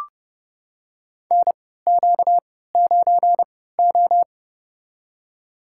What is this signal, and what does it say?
Morse code, heard as a keyed tone, about 32 wpm, spaced as for 17 wpm, 700 Hz. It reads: NQ9O
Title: USA Call Signs - 4 Characters - 3x Character Spacing - 30wpm